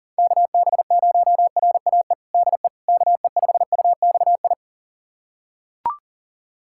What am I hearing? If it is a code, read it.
Morse code, heard as a keyed tone, about 40 wpm, 700 Hz. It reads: KB0RAE DE KE5UXI
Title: USA Callsign DE Callsign 40wpm